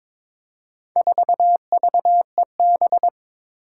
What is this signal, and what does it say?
Morse code, heard as a keyed tone, about 22 wpm, 700 Hz. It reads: 4VEB
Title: Four Challenging Letter-Number Combinations 22wpm